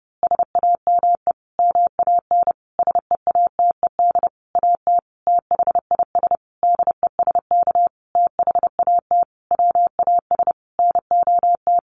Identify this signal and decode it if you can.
Morse code, heard as a keyed tone, about 30 wpm, 700 Hz. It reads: SAME MAN SEATED AT THIS DESK THAT WAS NOT
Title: A Sound Of Thunder 30wpm